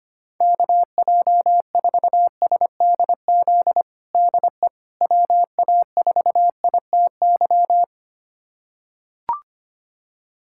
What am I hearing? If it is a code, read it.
Morse code, heard as a keyed tone, about 25 wpm, 700 Hz. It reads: KJ4SDZ DE WA4ITY
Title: USA Callsign DE Callsign - 6 Characters - 25wpm